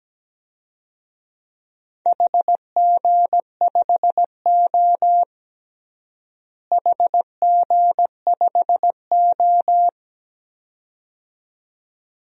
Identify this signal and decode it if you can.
Morse code, heard as a keyed tone, about 17 wpm, 700 Hz. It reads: HG5O HG5O
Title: DXCC-100 Call Signs - 4 Characters - Repeated 2x - 17wpm